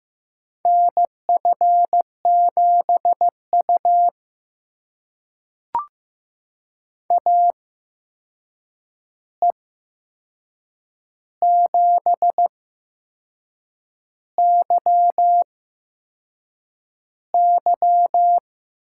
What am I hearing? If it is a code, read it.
Morse code, heard as a keyed tone, about 15 wpm, spaced as for 6 wpm, 700 Hz. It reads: NF7U AE7YY
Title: USA Call Signs - 8x Character Spacing - 15wpm